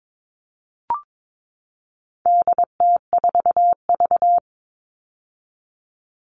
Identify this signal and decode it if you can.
Morse code, heard as a keyed tone, about 22 wpm, 700 Hz. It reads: DT4V
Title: All Cut Numbers - 4-Digit 22wpm